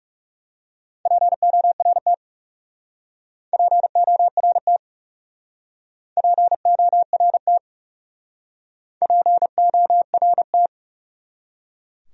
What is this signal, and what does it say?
Morse code, heard as a keyed tone, about 38 wpm, 700 Hz. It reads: PORT PORT PORT PORT